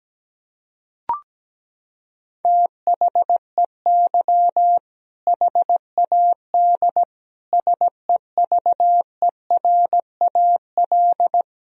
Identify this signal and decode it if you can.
Morse code, heard as a keyed tone, about 17 wpm, 700 Hz. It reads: THEY HAD SEVERAL
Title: Sentences from Top 1000 Words - Review Part 2 - 17wpm